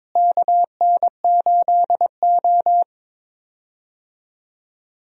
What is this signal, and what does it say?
Morse code, heard as a keyed tone, about 21 wpm, 700 Hz. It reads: KN8O
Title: USA Call Signs - 4 Characters - Repeated 2x - 22wpm